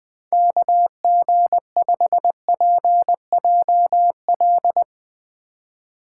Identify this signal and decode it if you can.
Morse code, heard as a keyed tone, about 20 wpm, 700 Hz. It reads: KG5PJL
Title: Speed Racing - USA 2x3 Call Signs 20wpm